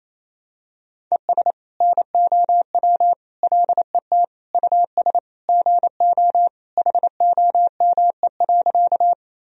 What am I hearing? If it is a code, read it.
Morse code, heard as a keyed tone, about 28 wpm, 700 Hz. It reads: ES NOW LET US GO HOME.
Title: Sentences from Top 300 Words and Abbreviations 28wpm